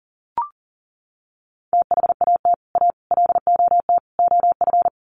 Morse code, heard as a keyed tone, about 40 wpm, 700 Hz. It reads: THAT A LOT OF
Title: Sets of 4 Words 40wpm